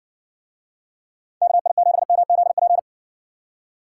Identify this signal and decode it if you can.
Morse code, heard as a keyed tone, about 64 wpm, 700 Hz. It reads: KI7MZP